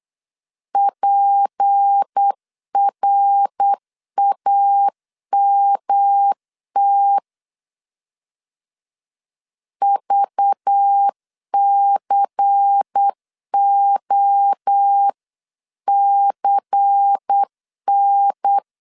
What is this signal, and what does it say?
Morse code, heard as a keyed tone, about 8 wpm, 800 Hz. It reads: PRAMT VCOCN
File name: LEZIONI / 6 PKX0 / 6 P 8wpm 3025